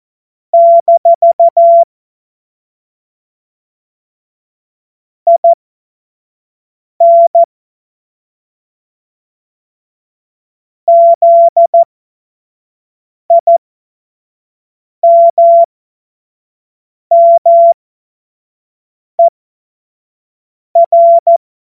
Morse code, heard as a keyed tone, about 14 wpm, spaced as for 5 wpm, 680 Hz. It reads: - IN ZIMMER
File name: Morsekurs / Witze / Witz 04. Juli 2023 25 F70